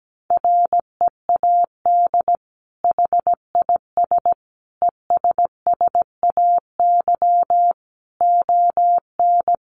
Morse code, heard as a keyed tone, about 17 wpm, 700 Hz. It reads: READ HIS ESSAY ON
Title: Sentences from Top 1500 Words 17wpm